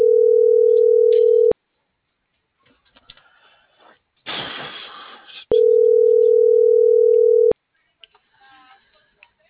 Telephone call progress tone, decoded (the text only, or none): ringback tone